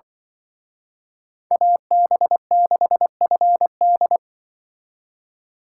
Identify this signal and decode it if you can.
Morse code, heard as a keyed tone, about 25 wpm, 700 Hz. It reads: AB6FD
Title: Speed Racing - USA Call Signs 20wpm